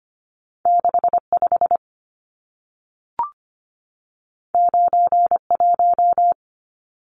Morse code, heard as a keyed tone, about 25 wpm, 700 Hz. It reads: 65 91
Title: Numbers - 2-Digit 25wpm